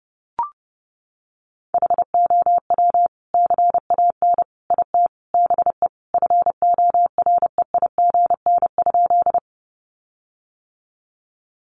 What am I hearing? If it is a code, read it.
Morse code, heard as a keyed tone, about 28 wpm, 700 Hz. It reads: HOW CAN IT BE FOREIGN?
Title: Sentences from Top 1400 Words 30wpm